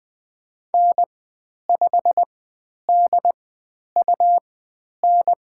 Morse code, heard as a keyed tone, about 20 wpm, spaced as for 10 wpm, 700 Hz. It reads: N5DUN
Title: 102 Lesson 17g - Letter u - Call Signs 20wpm